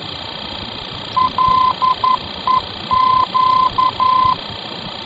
Morse code, heard as a keyed tone, about 11 wpm, 1020 Hz, with noise. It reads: LEQ
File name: france / mx / LEQ-F-VOR